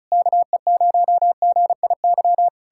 Morse code, heard as a keyed tone, about 41 wpm, 700 Hz. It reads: KE0GIY